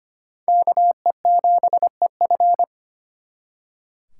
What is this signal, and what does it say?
Morse code, heard as a keyed tone, about 34 wpm, spaced as for 30 wpm, 700 Hz. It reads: KE7EF
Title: Speed Racing - USA 2x2 Call Signs 25wpm